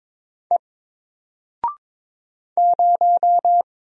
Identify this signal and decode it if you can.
Morse code, heard as a keyed tone, about 22 wpm, 700 Hz. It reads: E 0
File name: Single-Letter-Number-22wpm